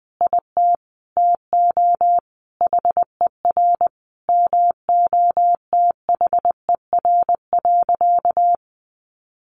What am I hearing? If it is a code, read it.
Morse code, heard as a keyed tone, about 19 wpm, 700 Hz. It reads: IT TO HER MOTHER.
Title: Sentences from Top 1500 Words 20wpm